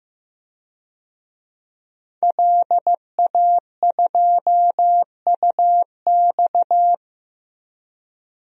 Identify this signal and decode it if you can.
Morse code, heard as a keyed tone, about 15 wpm, 700 Hz. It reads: LA2UX
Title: DXCC-100 Call Signs - 5 Characters - 15wpm